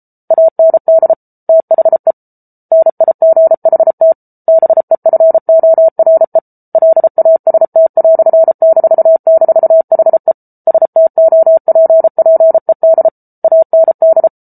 Morse code, heard as a keyed tone, about 34 wpm, 650 Hz. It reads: AND THE NIGHT BEFORE LAST"--HE STOPPED AND